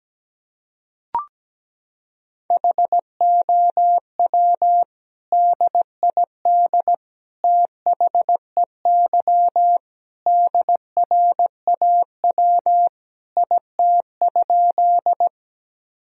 Morse code, heard as a keyed tone, about 17 wpm, 700 Hz. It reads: HOW DID THEY DRAW IT?
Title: Sentences from Top 1000 Words - Review Part 1 - 17wpm